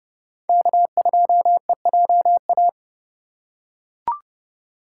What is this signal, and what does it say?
Morse code, heard as a keyed tone, about 30 wpm, 700 Hz. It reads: K2EJA
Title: USA Call Signs - 5 Characters - 2x Character Spacing - 30wpm